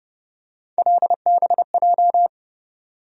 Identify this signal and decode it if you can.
Morse code, heard as a keyed tone, about 31 wpm, 700 Hz. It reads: LBJ
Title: USA Call Sign Postfixes - 3 Letters 30wpm